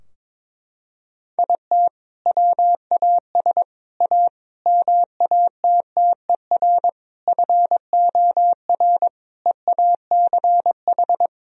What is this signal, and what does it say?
Morse code, heard as a keyed tone, about 23 wpm, 700 Hz. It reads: IT WAS A MATTER FOR EACH
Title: Sentences from Top 1000 Words - Review Part 4 - 22wpm